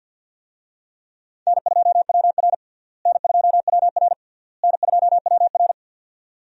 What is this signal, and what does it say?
Morse code, heard as a keyed tone, about 52 wpm, 700 Hz. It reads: N2WR N2WR N2WR
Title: USA Call Signs - 4 Characters - Repeated 3x - 50wpm